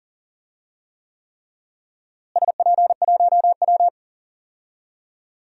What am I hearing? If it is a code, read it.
Morse code, heard as a keyed tone, about 40 wpm, 700 Hz. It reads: SP1W
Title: DXCC-100 Call Signs - 4 Characters - Repeated 3x - 40wpm